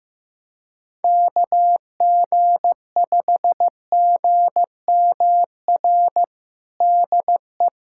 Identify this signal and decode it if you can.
Morse code, heard as a keyed tone, about 15 wpm, 700 Hz. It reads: KG5GMR DE
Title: USA Callsign DE Callsign 15wpm